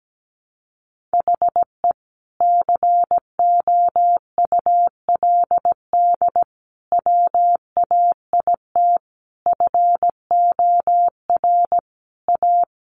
Morse code, heard as a keyed tone, about 17 wpm, 700 Hz. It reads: HE COULD WAIT FOR A
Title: Sentences from Top 1300 Words 17wpm